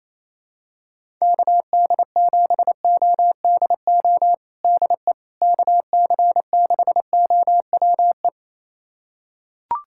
Morse code, heard as a keyed tone, about 27 wpm, 700 Hz. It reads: KD7ODO DE KC6OWE
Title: USA Callsign DE Callsign - 6 Characters - 28wpm